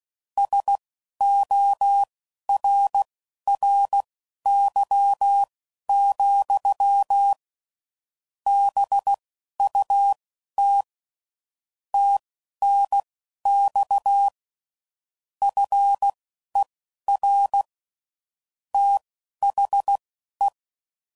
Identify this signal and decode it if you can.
Morse code, heard as a keyed tone, about 16 wpm, spaced as for 11 wpm, 800 Hz. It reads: SORRY, BUT TNX FER THE
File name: CW Code Groups @10WPM / QSOs / QSO kc7haa